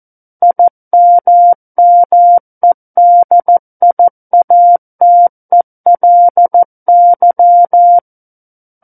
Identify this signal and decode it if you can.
Morse code, heard as a keyed tone, about 14 wpm, 700 Hz. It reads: IMMEDIATELY
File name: Book One / 05-Chapter 5 - 14 WPM